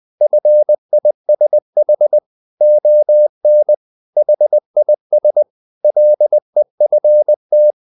Morse code, heard as a keyed tone, about 20 wpm, 600 Hz. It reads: FISH ON HIS LEFT
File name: the old man and the sea / the old man and the sea part 1 20wpm